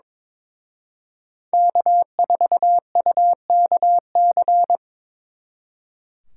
Morse code, heard as a keyed tone, about 26 wpm, 700 Hz. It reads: K4UKC
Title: Speed Racing - USA 1x3 Call Signs 22wpm